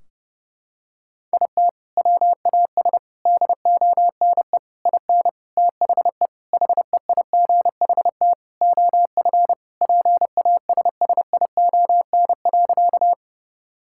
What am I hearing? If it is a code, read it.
Morse code, heard as a keyed tone, about 30 wpm, 700 Hz. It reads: IT WAS DONE IN THE HEIGHT OF PASSION.
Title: Sentences from Top 1600 Words 30wpm